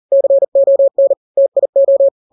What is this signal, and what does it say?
Morse code, heard as a keyed tone, about 40 wpm, 550 Hz. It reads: CON TIO
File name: n-grams / 3-grams 40 35 WPM